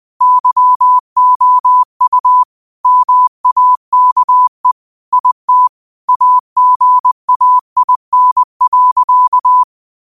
Morse code, heard as a keyed tone, about 20 wpm, 1000 Hz. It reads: YOU MAKE IT AGAIN.